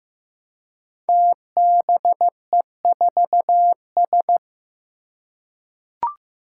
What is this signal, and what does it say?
Morse code, heard as a keyed tone, about 15 wpm, 700 Hz. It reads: TBE4S
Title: Five Challenging Letter-Number Combinations 15wpm